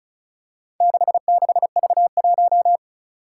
Morse code, heard as a keyed tone, about 36 wpm, 690 Hz. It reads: 66V1
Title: Four Challenging Letter-Number Combinations 35wpm